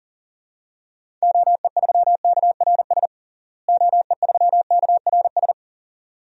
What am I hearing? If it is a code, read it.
Morse code, heard as a keyed tone, about 40 wpm, 700 Hz. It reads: OE3KRS OE3KRS
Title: DXCC-100 Call Signs - 6 Characters - Repeated 2x - 40wpm